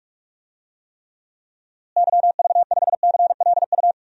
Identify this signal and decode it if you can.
Morse code, heard as a keyed tone, about 46 wpm, 700 Hz. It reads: YV5CLU
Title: DXCC-100 Call Signs - 6 Characters - Repeated 2x - 45wpm